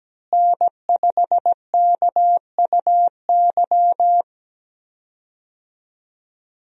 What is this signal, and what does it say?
Morse code, heard as a keyed tone, about 17 wpm, 700 Hz. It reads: N5KUY
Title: USA Call Signs - 5 Characters - Repeated 3x - 17wpm